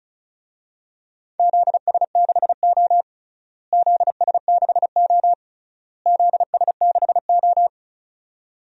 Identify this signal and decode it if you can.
Morse code, heard as a keyed tone, about 34 wpm, 700 Hz. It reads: ZS6O ZS6O ZS6O
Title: DXCC-100 Call Signs - 4 Characters - Repeated 3x - 35wpm